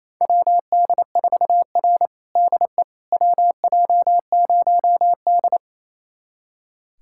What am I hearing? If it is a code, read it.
Morse code, heard as a keyed tone, about 27 wpm, 700 Hz. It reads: WD4R DE WJ0D
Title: USA Callsign DE Callsign - 4 Characters - 28wpm